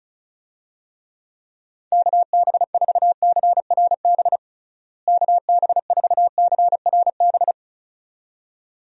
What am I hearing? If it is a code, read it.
Morse code, heard as a keyed tone, about 36 wpm, 700 Hz. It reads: KB4CRB KB4CRB
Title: USA Call Signs - 6 Characters - Repeated 2x - 35wpm